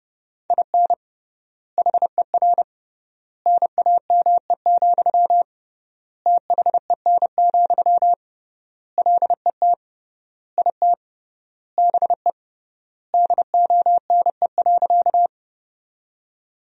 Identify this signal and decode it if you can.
Morse code, heard as a keyed tone, about 32 wpm, 700 Hz. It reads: IN HER NAME, THEN, LET IT BE DONE.
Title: 3x Word-Spacing - Sentences from Top 500 Words - 30wpm